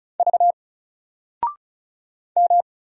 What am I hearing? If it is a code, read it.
Morse code, heard as a keyed tone, about 35 wpm, 700 Hz. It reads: V M
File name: Single-Letter-Number-35wpm